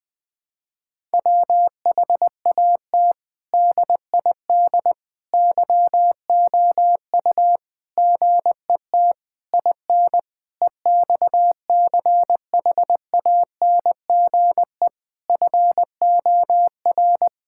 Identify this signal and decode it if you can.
Morse code, heard as a keyed tone, about 20 wpm, 700 Hz. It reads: WHAT DID YOU GET IN EXCHANGE FOR
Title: Sentences from Top 1000 Words - Review Part 6 - 20wpm